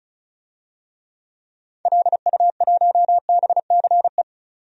Morse code, heard as a keyed tone, about 34 wpm, 700 Hz. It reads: LU1BCE
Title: DXCC-100 Call Signs - 6 Characters - 35wpm